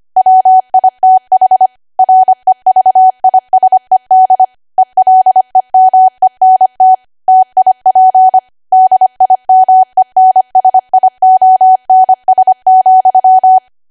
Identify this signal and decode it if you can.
Morse code, heard as a keyed tone, about 25 wpm, 750 Hz. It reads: WITH REVISED ELEMENT TIP DIMENSIONS,
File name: Archive / 25 WPM / 25WPM